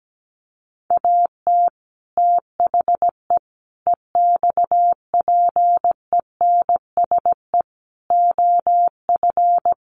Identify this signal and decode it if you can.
Morse code, heard as a keyed tone, about 17 wpm, 700 Hz. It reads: AT THE EXPENSE OF
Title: Sets of 4 Words 17wpm